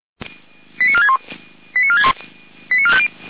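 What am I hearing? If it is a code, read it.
Five-tone selective call (ZVEI), heard as quick runs of five tones, tones of about 70 ms. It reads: 89461 89461 89450
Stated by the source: ZVEI five-tone selective call received by SDR.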